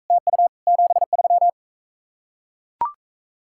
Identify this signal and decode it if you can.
Morse code, heard as a keyed tone, about 43 wpm, 700 Hz. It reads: TU 73